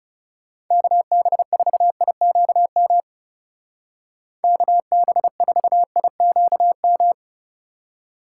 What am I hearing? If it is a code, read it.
Morse code, heard as a keyed tone, about 35 wpm, 700 Hz. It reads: KB4IQM KB4IQM
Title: Speed Racing - USA 2x3 Call Signs 30wpm